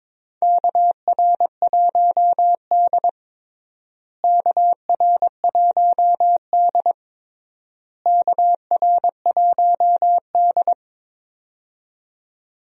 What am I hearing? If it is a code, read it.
Morse code, heard as a keyed tone, about 22 wpm, 700 Hz. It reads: KR1D KR1D KR1D
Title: USA Call Signs - 4 Characters - Repeated 3x - 22wpm